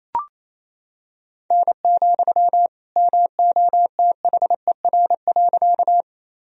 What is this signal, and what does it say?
Morse code, heard as a keyed tone, about 28 wpm, 700 Hz. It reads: N, MOTHER.
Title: Sentences from Top 200 Words and Abbreviations 28wpm